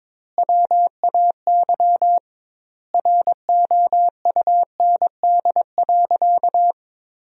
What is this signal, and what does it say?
Morse code, heard as a keyed tone, about 22 wpm, 700 Hz. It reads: WAY ROUND.
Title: 2x Word-Spacing - Sentences from Top 200 Words - 22wpm